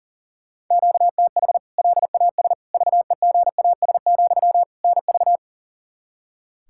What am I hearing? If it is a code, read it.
Morse code, heard as a keyed tone, about 37 wpm, 700 Hz. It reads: QTH LAS VEGAS, NV
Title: K1USN SST QSO Practice 40wpm